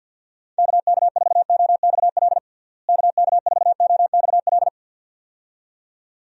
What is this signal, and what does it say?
Morse code, heard as a keyed tone, about 51 wpm, 700 Hz. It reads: KK4OXL KK4OXL
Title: USA Call Signs - 6 Characters - Repeated 2x - 50wpm